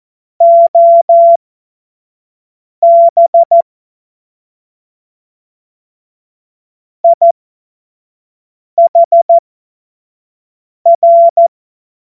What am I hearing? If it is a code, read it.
Morse code, heard as a keyed tone, about 14 wpm, spaced as for 5 wpm, 680 Hz. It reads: OB IHR
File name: Morsekurs / Witze / Witz 02. Februar 2021, 25 F70